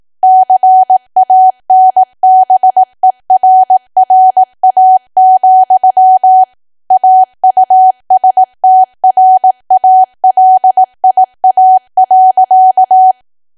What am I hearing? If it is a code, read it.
Morse code, heard as a keyed tone, about 18 wpm, 750 Hz. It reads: CANBERRA, AUSTRALIA.
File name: cwa / W1AW-Level2 / 18WPM